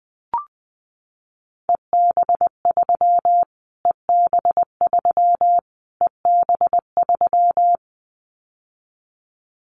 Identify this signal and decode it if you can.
Morse code, heard as a keyed tone, about 20 wpm, 700 Hz. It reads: EB3 EB3 EB3
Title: Callsign Prefixes for Top 25 DXCC - Repeated 3x - 20wpm